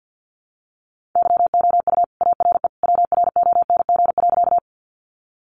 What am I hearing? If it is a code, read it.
Morse code, heard as a keyed tone, about 50 wpm, 700 Hz. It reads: YOU ARE WRONG.
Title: Sentences from Top 900 Words 50wpm